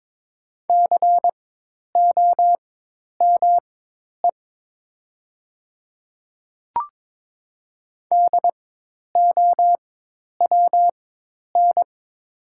Morse code, heard as a keyed tone, about 22 wpm, spaced as for 10 wpm, 700 Hz. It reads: COME DOWN
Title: 4x Character Spacing - Top 100 Words - 22wpm